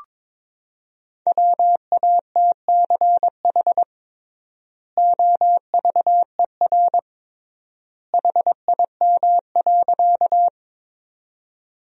Morse code, heard as a keyed tone, about 22 wpm, 700 Hz. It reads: WATCH OVER HIM.
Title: 3x Word-Spacing - Sentences from Top 300 Words - 22wpm